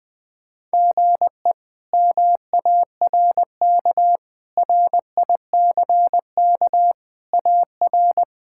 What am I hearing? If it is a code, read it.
Morse code, heard as a keyed tone, about 20 wpm, 700 Hz. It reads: GE MARK RICK AR
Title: K1USN SST QSO Practice - 20wpm